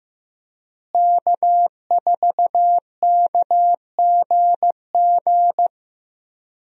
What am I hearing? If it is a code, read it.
Morse code, heard as a keyed tone, about 16 wpm, 700 Hz. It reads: K4KGG
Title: USA Call Signs - 5 Characters - Repeated 2x - 15wpm